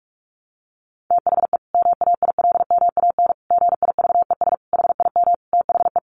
Morse code, heard as a keyed tone, about 44 wpm, 700 Hz. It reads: THE MAILMAN GIVES HIM THE
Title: Easy Phrases 45wpm